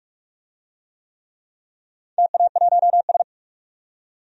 Morse code, heard as a keyed tone, about 44 wpm, 700 Hz. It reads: TA1S
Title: DXCC-100 Call Signs - 4 Characters - 45wpm